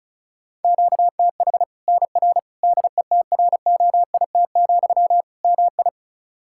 Morse code, heard as a keyed tone, about 34 wpm, 700 Hz. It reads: QTH NR DETROIT, MI
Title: Practice QSO - Exchange 1 - Beginning - 35wpm